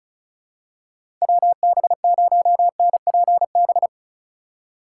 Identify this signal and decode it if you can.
Morse code, heard as a keyed tone, about 36 wpm, 700 Hz. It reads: WB0NPB
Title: USA Call Signs - 6 Characters - 2x Character Spacing - 35wpm